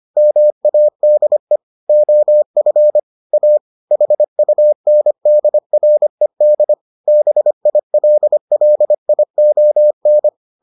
Morse code, heard as a keyed tone, about 25 wpm, 600 Hz. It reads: MADE OF A HUNDRED BILLION